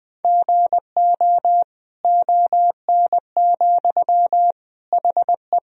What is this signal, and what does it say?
Morse code, heard as a keyed tone, about 20 wpm, 700 Hz. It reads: GO ON, HE
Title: Sentences from Top 100 Words and Abbreviations 20wpm